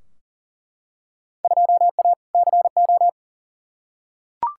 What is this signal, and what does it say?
Morse code, heard as a keyed tone, about 40 wpm, 700 Hz. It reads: 2A CO
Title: Field Day Exchanges - 40wpm